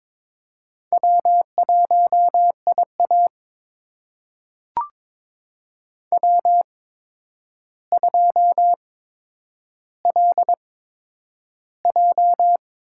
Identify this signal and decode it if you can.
Morse code, heard as a keyed tone, about 22 wpm, spaced as for 9 wpm, 700 Hz. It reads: W1IA W2LJ
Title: USA Call Signs - 4 Characters - 8x Character Spacing - 22wpm